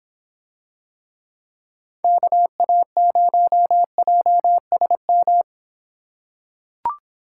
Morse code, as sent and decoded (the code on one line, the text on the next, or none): -.- .- ----- .--- ... --
KA0JSM